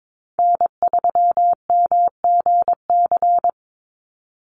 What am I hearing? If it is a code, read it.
Morse code, heard as a keyed tone, about 22 wpm, 700 Hz. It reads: N3MGC
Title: USA Call Signs 22wpm